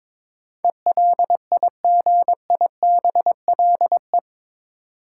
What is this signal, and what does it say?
Morse code, heard as a keyed tone, about 22 wpm, 700 Hz. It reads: ELIGIBLE
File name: practice / 22wpm / Eight Letter Words - 22wpm